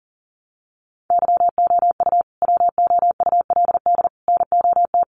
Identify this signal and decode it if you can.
Morse code, heard as a keyed tone, about 40 wpm, 700 Hz. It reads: YOU WOULD NOT
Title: Sets of 5 Words Encore2 40wpm